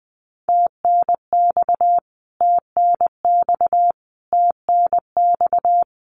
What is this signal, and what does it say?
Morse code, heard as a keyed tone, about 20 wpm, 700 Hz. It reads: TNX TNX TNX
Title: Abbreviations Repeated 3x - 20wpm